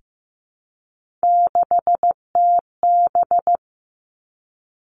Morse code, heard as a keyed tone, about 15 wpm, 700 Hz. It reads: 6TB
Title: Three Challenging Letter-Number Combinations 15wpm